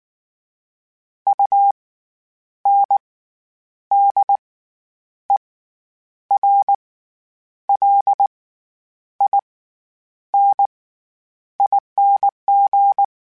Morse code, heard as a keyed tone, about 19 wpm, spaced as for 9 wpm, 800 Hz. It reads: UNDERLINING
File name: Special INGs / Special INGs-mp3s / SlowToFaster ing @18